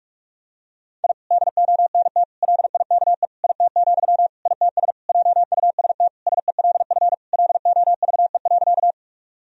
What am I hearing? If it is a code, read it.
Morse code, heard as a keyed tone, about 42 wpm, 700 Hz. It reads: I DONT LIKE IT, ITS JUST SELF LOVE.
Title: Sentences from Top 1000 Words - Review Part 1 - 45wpm